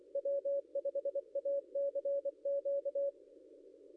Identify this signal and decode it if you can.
Morse code, heard as a keyed tone, about 23 wpm, 560 Hz. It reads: W5ACQ
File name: contests / ss-cw-k5tr-op-n5rz -0800